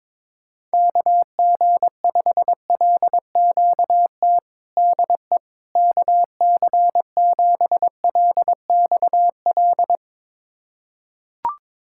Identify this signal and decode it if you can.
Morse code, heard as a keyed tone, about 22 wpm, 700 Hz. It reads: KG5LQT DE KC7LXL
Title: USA Callsign DE Callsign - 6 Characters - 22wpm